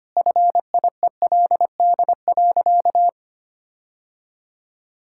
Morse code, heard as a keyed tone, about 25 wpm, 700 Hz. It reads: FIELD.
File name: practice / 25wpm / All-Age Appropriate Jokes - 25wpm